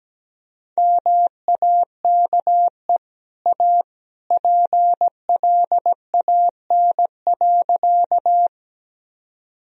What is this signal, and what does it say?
Morse code, heard as a keyed tone, about 17 wpm, 700 Hz. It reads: MAKE A PLAN.
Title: Sentences from Top 1000 Words - Review Part 2 - 17wpm